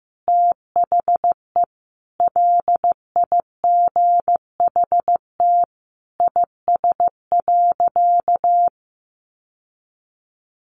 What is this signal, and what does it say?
Morse code, heard as a keyed tone, about 15 wpm, 700 Hz. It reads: THE LIGHT IS.
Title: Sentences from Top 1000 Words 15wpm